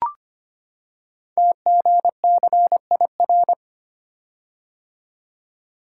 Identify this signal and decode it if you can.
Morse code, heard as a keyed tone, about 25 wpm, 700 Hz. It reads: TGCIR
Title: Code Groups - 5 Letters - 25wpm